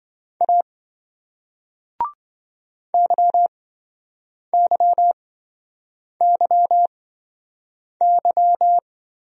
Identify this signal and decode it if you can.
Morse code, heard as a keyed tone, about 27 wpm, spaced as for 7 wpm, 700 Hz. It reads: A YYYY